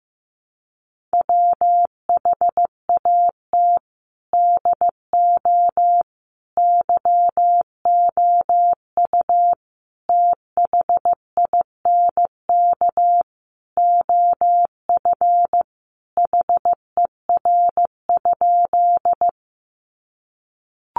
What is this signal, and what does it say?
Morse code, heard as a keyed tone, about 15 wpm, 700 Hz. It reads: WHAT DO YOU THINK OF HER?
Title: Sentences from Top 200 Words 15wpm